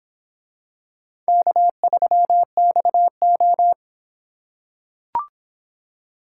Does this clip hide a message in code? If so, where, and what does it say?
Yes: Morse code from 1.3 to 3.7 s, heard as a keyed tone, about 25 wpm, 700 Hz. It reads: K3XO